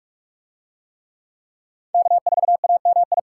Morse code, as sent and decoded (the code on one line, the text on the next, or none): -.- ....- .- -- ..
K4AMI